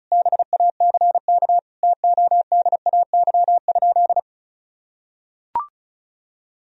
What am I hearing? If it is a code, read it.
Morse code, heard as a keyed tone, about 34 wpm, 700 Hz. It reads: BACK TODAY?
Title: Sentences from Top 1000 Words - Review Part 4 - 35wpm